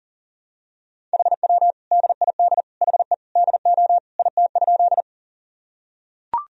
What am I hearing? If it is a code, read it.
Morse code, heard as a keyed tone, about 40 wpm, 700 Hz. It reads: HW DID HE DO IT?
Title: Sentences from Top 100 Words and Abbreviations 40wpm